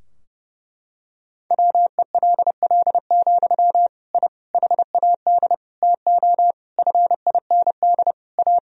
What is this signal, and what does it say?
Morse code, heard as a keyed tone, about 32 wpm, 700 Hz. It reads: WELL, I HAD TO FIND A
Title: Sentences from Top 1000 Words - Review Part 3 - 30wpm